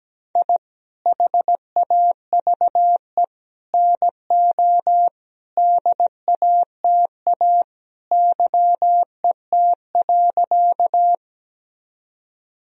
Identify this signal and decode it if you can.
Morse code, heard as a keyed tone, about 17 wpm, 700 Hz. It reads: I HAVE NO DATA YET.
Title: Sentences from Top 1000 Words - Review Part 3 - 17wpm